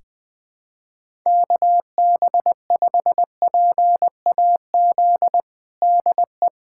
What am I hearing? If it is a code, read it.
Morse code, heard as a keyed tone, about 21 wpm, 700 Hz. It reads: KB5PAZ DE
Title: USA Callsign DE Callsign 20wpm